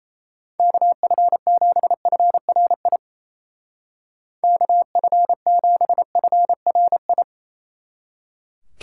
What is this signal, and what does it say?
Morse code, heard as a keyed tone, about 37 wpm, 700 Hz. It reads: KF7FRI KF7FRI